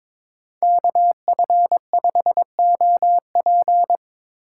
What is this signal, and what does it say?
Morse code, heard as a keyed tone, about 26 wpm, 700 Hz. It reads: KF5OP
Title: Speed Racing - USA 2x2 Call Signs 22wpm